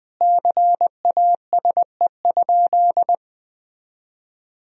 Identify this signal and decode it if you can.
Morse code, heard as a keyed tone, about 20 wpm, 700 Hz. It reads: CASE?
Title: Sentences from Top 1000 Words - Review Part 3 - 20wpm